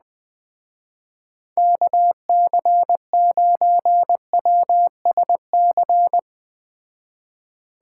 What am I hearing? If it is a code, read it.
Morse code, heard as a keyed tone, about 20 wpm, 700 Hz. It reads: KC9WSC
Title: Speed Racing - USA 2x3 Call Signs 17wpm